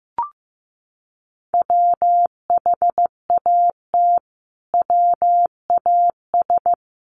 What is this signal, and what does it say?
Morse code, heard as a keyed tone, about 15 wpm, 700 Hz. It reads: WHAT WAS
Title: Sentences from Top 1400 Words 15wpm